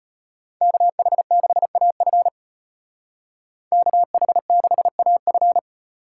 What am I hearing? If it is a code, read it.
Morse code, heard as a keyed tone, about 33 wpm, 700 Hz. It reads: KH6AF KH6AF